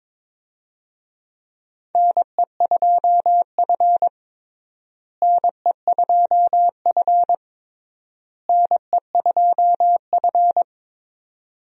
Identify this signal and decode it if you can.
Morse code, heard as a keyed tone, about 22 wpm, 700 Hz. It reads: NE2F NE2F NE2F
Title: USA Call Signs - 4 Characters - Repeated 3x - 22wpm